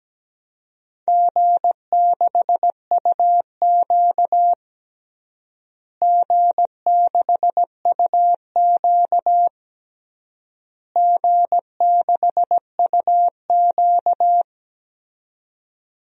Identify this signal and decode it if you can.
Morse code, heard as a keyed tone, about 17 wpm, 700 Hz. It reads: G6UQ G6UQ G6UQ
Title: DXCC-100 Call Signs - 4 Characters - Repeated 3x - 17wpm